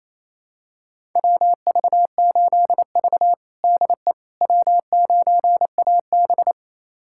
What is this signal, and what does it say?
Morse code, heard as a keyed tone, about 27 wpm, 700 Hz. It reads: WV8V DE W9AB
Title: USA Callsign DE Callsign - 4 Characters - 28wpm